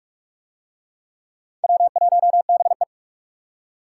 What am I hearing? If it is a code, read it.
Morse code, heard as a keyed tone, about 45 wpm, 700 Hz. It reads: W1BE